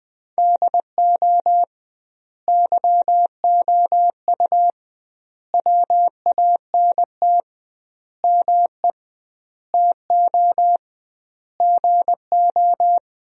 Morse code, heard as a keyed tone, about 20 wpm, 700 Hz. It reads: DO YOU WANT ME TO GO
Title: 2x Word-Spacing - Sentences from Top 200 Words - 20wpm